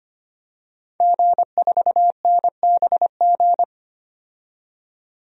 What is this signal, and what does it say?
Morse code, heard as a keyed tone, about 25 wpm, 700 Hz. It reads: G4NBG
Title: DXCC-100 Call Signs - 5 Characters - 25wpm